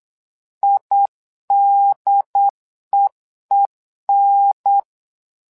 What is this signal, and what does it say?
Morse code, heard as a keyed tone, about 8 wpm, 800 Hz. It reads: IDEEN
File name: LEZIONI / 2 UHDN / 2 UHDN 8wpm2232